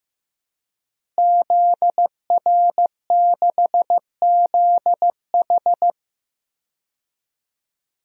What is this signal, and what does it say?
Morse code, heard as a keyed tone, about 15 wpm, 700 Hz. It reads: ZR6ZH
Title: DXCC-100 Call Signs - 5 Characters - Repeated 3x - 15wpm